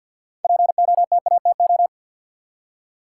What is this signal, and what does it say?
Morse code, heard as a keyed tone, about 48 wpm, 700 Hz. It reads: POTATO
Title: Six Letter Words - 50wpm